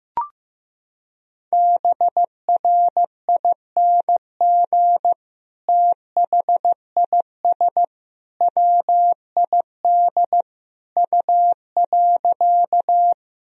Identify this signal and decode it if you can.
Morse code, heard as a keyed tone, about 15 wpm, 700 Hz. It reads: BRING THIS WID U.
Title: Sentences from Top 500 Words and Abbreviations 15wpm